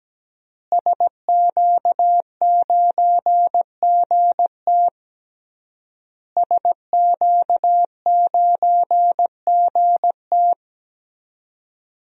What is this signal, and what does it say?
Morse code, heard as a keyed tone, about 17 wpm, 700 Hz. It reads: SQ9GT SQ9GT
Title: DXCC-100 Call Signs - 5 Characters - Repeated 3x - 17wpm